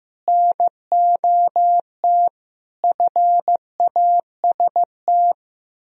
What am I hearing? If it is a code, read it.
Morse code, heard as a keyed tone, about 16 wpm, 700 Hz. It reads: NOT FAST
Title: Sentences from Top 1000 Words - Review Part 3 - 15wpm